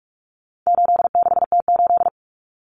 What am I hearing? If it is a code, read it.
Morse code, heard as a keyed tone, about 44 wpm, 700 Hz. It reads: 86T8
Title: Common Cut Numbers - 4-Digit 45wpm